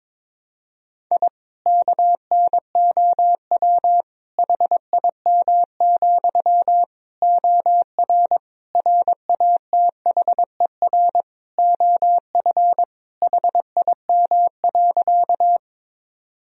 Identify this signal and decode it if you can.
Morse code, heard as a keyed tone, about 21 wpm, 700 Hz. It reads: I KNOW HIM, OR RATHER OF HIM.
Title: Sentences from Top 1000 Words - Review Part 4 - 22wpm